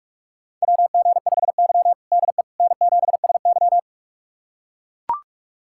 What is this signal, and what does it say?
Morse code, heard as a keyed tone, about 44 wpm, 690 Hz. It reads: WG5Y DE N7SY
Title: USA Callsign DE Callsign - 4 Characters - 45wpm